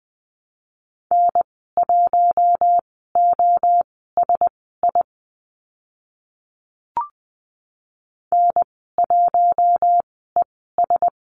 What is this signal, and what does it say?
Morse code, heard as a keyed tone, about 20 wpm, spaced as for 15 wpm, 700 Hz. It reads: N1OSI N1ES
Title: 042 Lesson 8g - Number 1 - Call Signs 20wpm